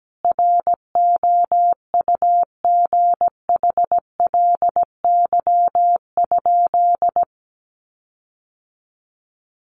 Morse code, heard as a keyed tone, about 17 wpm, 700 Hz. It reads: ROUGHLY?
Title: Sentences from Top 1500 Words 17wpm